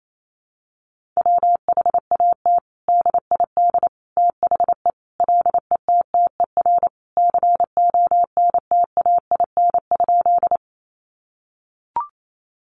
Morse code, heard as a keyed tone, about 28 wpm, 700 Hz. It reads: WHAT DID THE LETTER CONTAIN?
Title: Sentences from Top 500 Words 28wpm